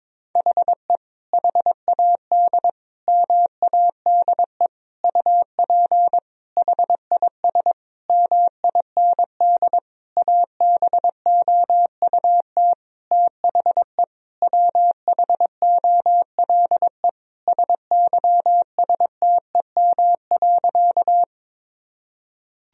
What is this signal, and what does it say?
Morse code, heard as a keyed tone, about 22 wpm, 700 Hz. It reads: HE HAD MADE UP HIS MIND ABOUT THE WHOLE SYSTEM.
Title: Sentences from Top 1000 Words - Review Part 3 - 22wpm